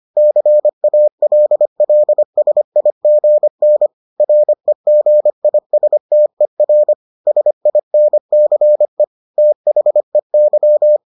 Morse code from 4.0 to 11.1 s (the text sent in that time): REGISTER SINCE THEY